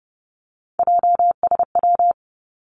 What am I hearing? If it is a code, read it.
Morse code, heard as a keyed tone, about 30 wpm, 700 Hz. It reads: JSW
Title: Three Challenging Letter-Number Combinations 30wpm